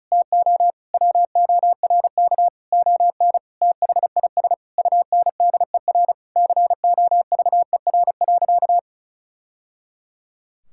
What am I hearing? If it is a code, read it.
Morse code, heard as a keyed tone, about 34 wpm, 700 Hz. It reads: TO WORK ON THIS UNDER COVER.
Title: Sentences from Top 1000 Words - Review Part 6 - 35wpm